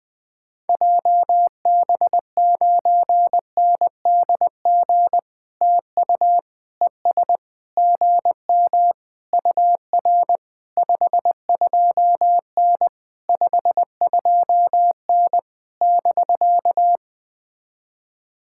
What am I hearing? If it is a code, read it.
Morse code, heard as a keyed tone, about 20 wpm, 700 Hz. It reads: JB9NDG TU ES GM UR 52N 52N <BK>
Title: POTA QSO Practice - 20wpm